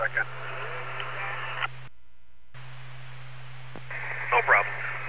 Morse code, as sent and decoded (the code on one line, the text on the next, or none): none